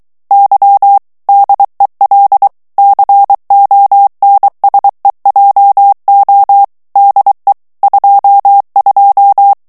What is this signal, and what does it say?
Morse code, heard as a keyed tone, about 23 wpm, 800 Hz. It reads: Y DEL CONSEJO DE 22
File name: morse mp3 / ESPAGNOL TEXTE EN CLAIR 2 VITESSE 22 MOTS MINUTES